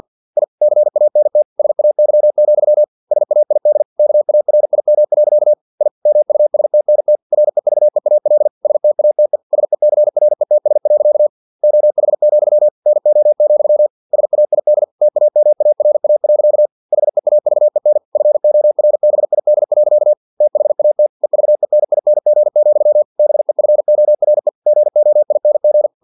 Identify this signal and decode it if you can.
Morse code, heard as a keyed tone, about 50 wpm, 600 Hz. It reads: I CANT SAY, SAID KARIM. I MUSTNT REVEAL STATE SECRETS. OH, NO, SAID TAMARA. HEAVEN FORBID. THAT EVENING, BEFORE GOING
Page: Never 49wpm